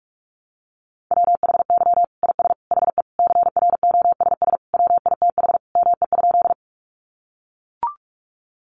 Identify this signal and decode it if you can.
Morse code, heard as a keyed tone, about 44 wpm, 700 Hz. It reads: WHY IS HE CROSS WITH ME?
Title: Sentences from Top 300 Words 45wpm